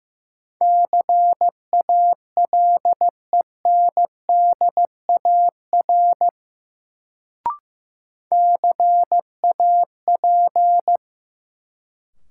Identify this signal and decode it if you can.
Morse code, heard as a keyed tone, about 16 wpm, 700 Hz. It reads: CALENDAR CAP
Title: Sentences from Top 1600 Words - New Word Drill - 15wpm